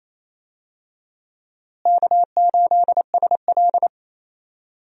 Morse code, heard as a keyed tone, about 28 wpm, 700 Hz. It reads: K8SL
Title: USA Call Signs - 4 Characters - Repeated 2x - 28wpm